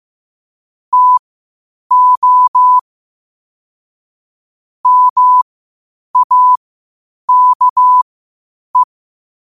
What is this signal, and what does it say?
Morse code, heard as a keyed tone, about 15 wpm, spaced as for 8 wpm, 1000 Hz. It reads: TO MAKE
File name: qotdfiles / 8wpm